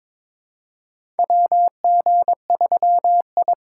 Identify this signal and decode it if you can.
Morse code, heard as a keyed tone, about 22 wpm, 700 Hz. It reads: WG3I
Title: USA Call Signs - 4 Characters - 6x Character Spacing - 22wpm